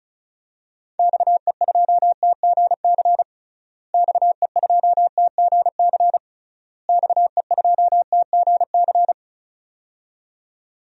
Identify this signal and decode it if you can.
Morse code, heard as a keyed tone, about 36 wpm, 700 Hz. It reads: XE2TGC XE2TGC XE2TGC
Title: DXCC-100 Call Signs - 6 Characters - Repeated 3x - 35wpm